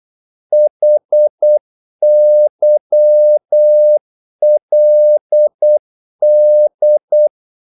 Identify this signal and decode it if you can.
Morse code, heard as a keyed tone, about 8 wpm, 600 Hz. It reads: HYLD
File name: Korte ord DK 8wpm